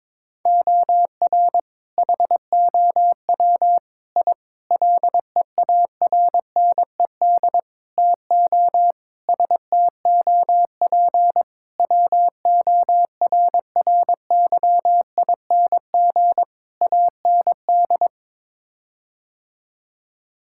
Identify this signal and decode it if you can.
Morse code, heard as a keyed tone, about 22 wpm, 700 Hz. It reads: OR HOW I LEARNED TO STOP WORRYING AND
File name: practice / 22wpm / Sets of 8 Words - 22wpm